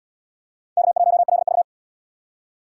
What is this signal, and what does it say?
Morse code, heard as a keyed tone, about 77 wpm, 700 Hz. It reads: K1LV